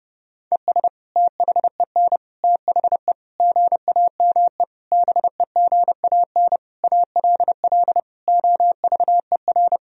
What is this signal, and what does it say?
Morse code, heard as a keyed tone, about 32 wpm, 700 Hz. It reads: ES THEN THE GAME BEGAN ALL OVER
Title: Sentences from Top 500 Words and Abbreviations 30wpm